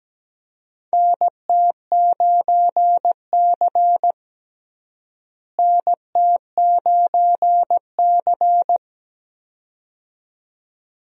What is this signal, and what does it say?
Morse code, heard as a keyed tone, about 17 wpm, 700 Hz. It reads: NT9C NT9C
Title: USA Call Signs - 4 Characters - Repeated 2x - 17wpm